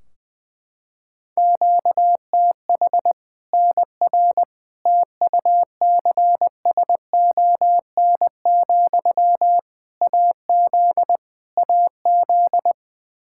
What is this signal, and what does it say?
Morse code, heard as a keyed tone, about 20 wpm, 700 Hz. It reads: QTH NR TUCSON, AZ AZ
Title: Practice QSO - Exchange 1 - Beginning - 20wpm